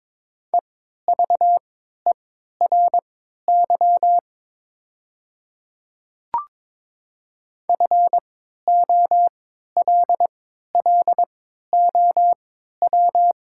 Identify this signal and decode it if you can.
Morse code, heard as a keyed tone, about 22 wpm, spaced as for 13 wpm, 700 Hz. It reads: EVERY FOLLOW
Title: 3x Character Spacing - Top 200 Words - 22wpm